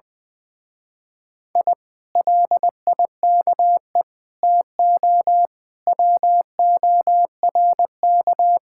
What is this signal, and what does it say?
Morse code, heard as a keyed tone, about 21 wpm, 700 Hz. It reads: I LIKE TO WORK
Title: Speed-Racing Common QSO Phrases - 20wpm